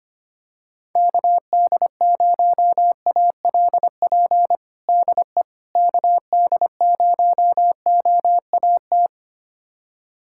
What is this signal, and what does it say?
Morse code, heard as a keyed tone, about 25 wpm, 700 Hz. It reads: KD0ALP DE KD0OAT
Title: USA Callsign DE Callsign 25wpm